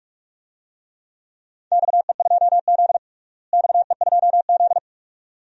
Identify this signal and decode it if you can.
Morse code, heard as a keyed tone, about 46 wpm, 700 Hz. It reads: XE2Z XE2Z
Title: DXCC-100 Call Signs - 4 Characters - Repeated 2x - 45wpm